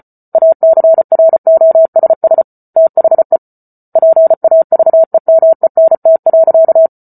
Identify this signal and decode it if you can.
Morse code, heard as a keyed tone, about 34 wpm, 650 Hz. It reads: ACROSS THE PAVEMENT.